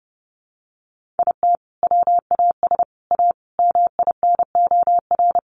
Morse code, heard as a keyed tone, about 30 wpm, 700 Hz. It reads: IT WAS A MINOR
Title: Sentences from Top 1500 Words 30wpm